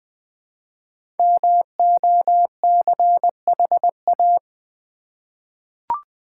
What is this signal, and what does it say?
Morse code, heard as a keyed tone, about 20 wpm, 690 Hz. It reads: MOCHA
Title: Easy Words with Hints - 20wpm